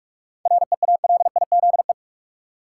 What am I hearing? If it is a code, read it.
Morse code, heard as a keyed tone, about 47 wpm, 700 Hz. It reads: REALIZE
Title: Seven Letter Words - 45wpm